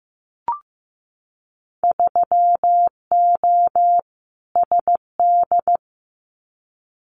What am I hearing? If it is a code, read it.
Morse code, heard as a keyed tone, about 15 wpm, 700 Hz. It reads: 3O SD
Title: Winter Field Day Exchanges 15wpm